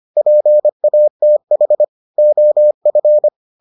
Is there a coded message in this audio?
Yes: Morse code from 0.2 to 3.3 s, heard as a keyed tone, about 25 wpm, 600 Hz. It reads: PATH OF